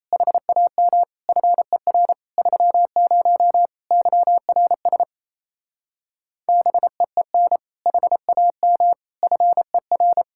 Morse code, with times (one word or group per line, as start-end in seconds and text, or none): 0.1-1.0 HAM
1.3-2.1 FER
2.4-3.7 30
3.9-5.0 YRS
6.5-7.6 BEEN
7.9-8.9 HAM
9.2-10.2 FER